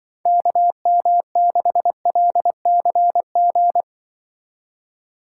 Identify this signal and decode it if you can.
Morse code, heard as a keyed tone, about 24 wpm, 700 Hz. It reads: KM6LCG
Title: Speed Racing - USA 2x3 Call Signs 20wpm